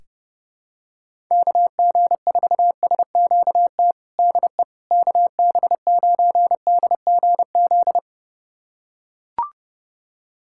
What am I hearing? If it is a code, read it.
Morse code, heard as a keyed tone, about 30 wpm, 700 Hz. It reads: KG4SQT DE KB9DGZ
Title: USA Callsign DE Callsign - 6 Characters - 30wpm